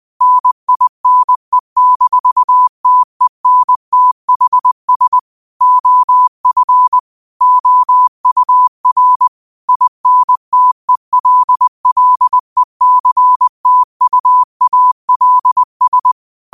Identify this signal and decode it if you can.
Morse code, heard as a keyed tone, about 20 wpm, 1000 Hz. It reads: NINE-TENTHS OF OUR INTELLECTUALS
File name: qotdfiles / 20wpm